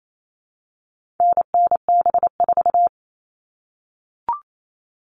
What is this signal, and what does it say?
Morse code, heard as a keyed tone, about 28 wpm, 700 Hz. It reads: NNB4
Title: All Cut Numbers - 4-Digit 28wpm